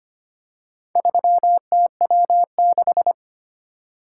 Morse code, heard as a keyed tone, about 25 wpm, 700 Hz. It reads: 3TW6
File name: practice / 25wpm / Four Letter-Number Combinations - 25wpm